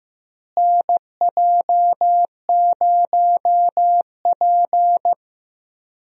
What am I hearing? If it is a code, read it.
Morse code, heard as a keyed tone, about 15 wpm, 700 Hz. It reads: NJ0P
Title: USA Call Signs - 4 Characters - Repeated 3x - 15wpm